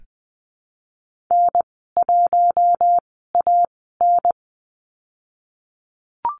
Morse code, heard as a keyed tone, about 20 wpm, spaced as for 14 wpm, 700 Hz. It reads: N1AN
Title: 048 Lesson 9g - Number 4 - Call Signs 20wpm